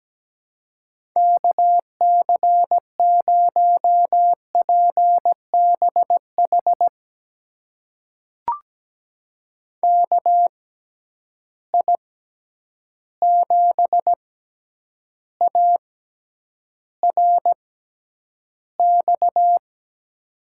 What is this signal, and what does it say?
Morse code, heard as a keyed tone, about 17 wpm, spaced as for 9 wpm, 700 Hz. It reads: KC0PBH KI7ARX
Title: USA Call Signs - 6 Characters - 6x Character Spacing - 17wpm